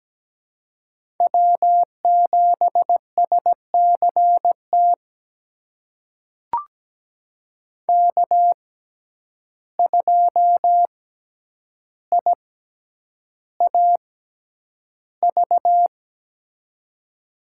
Morse code, heard as a keyed tone, about 17 wpm, spaced as for 9 wpm, 700 Hz. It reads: W7SCT K2IAV
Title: USA Call Signs - 5 Characters - 6x Character Spacing - 17wpm